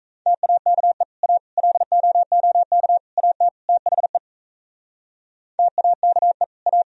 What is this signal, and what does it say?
Morse code, heard as a keyed tone, about 36 wpm, 700 Hz. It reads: TAKE A LOOK AT THE TAKE A